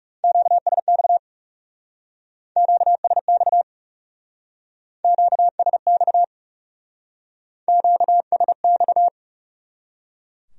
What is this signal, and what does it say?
Morse code, heard as a keyed tone, about 35 wpm, 700 Hz. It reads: QSX QSX QSX QSX
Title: Speed-Racing Q-Codes - 30wpm